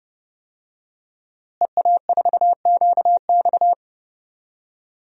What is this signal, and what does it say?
Morse code, heard as a keyed tone, about 32 wpm, 700 Hz. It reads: EA4QX
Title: DXCC-100 Call Signs - 5 Characters - Repeated 3x - 30wpm